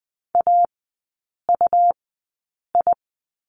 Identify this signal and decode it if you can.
Morse code, heard as a keyed tone, about 20 wpm, spaced as for 8 wpm, 700 Hz. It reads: AUI
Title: 152 Lesson 25e - Letter p - Three Characters 20wpm